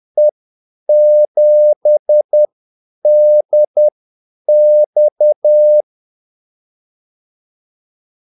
Morse code, heard as a keyed tone, about 10 wpm, spaced as for 8 wpm, 600 Hz. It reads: E7DX
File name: Callsigns-10-8wpm